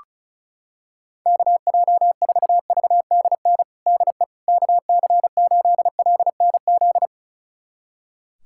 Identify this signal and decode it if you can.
Morse code, heard as a keyed tone, about 34 wpm, 700 Hz. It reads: KJ4VDN DE KC8LNZ
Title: USA Callsign DE Callsign - 6 Characters - 35wpm